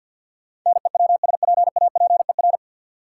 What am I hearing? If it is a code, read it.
Morse code, heard as a keyed tone, about 50 wpm, 700 Hz. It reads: NEWSPAPER
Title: Sentences from Top 1000 Words - Review Part 4 - New Word Drill - 50wpm